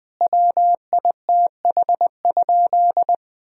Morse code, heard as a keyed tone, about 19 wpm, 700 Hz. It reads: WITH?
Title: Sentences from Top 1000 Words - Review Part 4 - 20wpm